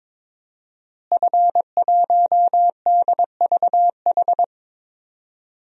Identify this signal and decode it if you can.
Morse code, heard as a keyed tone, about 22 wpm, 700 Hz. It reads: F1DVH
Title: DXCC-100 Call Signs - 5 Characters - Repeated 2x - 22wpm